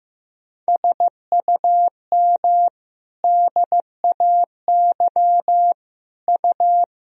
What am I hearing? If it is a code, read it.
Morse code, heard as a keyed tone, about 15 wpm, 700 Hz. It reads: SUM DAY U
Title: Sentences from Top 200 Words and Abbreviations 15wpm